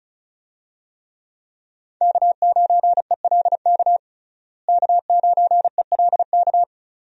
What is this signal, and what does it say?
Morse code, heard as a keyed tone, about 36 wpm, 700 Hz. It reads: K9ELK K9ELK
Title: USA Call Signs - 5 Characters - Repeated 2x - 35wpm